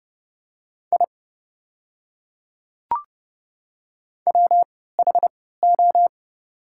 Morse code, heard as a keyed tone, about 30 wpm, spaced as for 17 wpm, 700 Hz. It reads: I WHO
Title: 3x Character Spacing - Top 100 Words - 30wpm